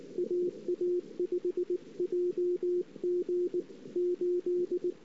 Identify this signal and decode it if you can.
Morse code, heard as a keyed tone, about 19 wpm, 360 Hz. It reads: AA5JG 8
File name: CW / 2008-ss-cw-k5tr-op-n5rz -06:00